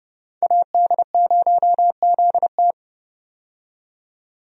Morse code, heard as a keyed tone, about 31 wpm, 700 Hz. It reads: AD0ZT
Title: USA Call Signs - 5 Characters - 30wpm